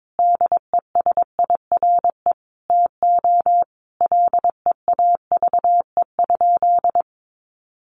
Morse code, heard as a keyed tone, about 22 wpm, 700 Hz. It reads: DESIRE TO LEAVE?
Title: Sentences from Top 1100 Words 22wpm